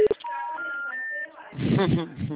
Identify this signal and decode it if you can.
Special information tone, with speech.
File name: out- -2007- - - .4895